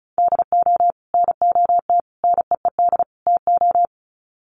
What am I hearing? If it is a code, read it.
Morse code, heard as a keyed tone, about 34 wpm, 700 Hz. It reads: DO NOT NEED TO
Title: Sets of 4 Words 35wpm